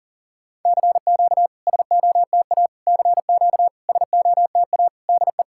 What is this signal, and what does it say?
Morse code, heard as a keyed tone, about 39 wpm, 690 Hz. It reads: CQ SOTA CQ SOTA DE
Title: SOTA QSO Practice 40wpm